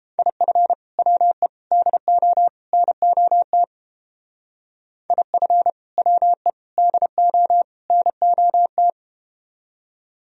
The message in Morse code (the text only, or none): IF WE DO NOT IF WE DO NOT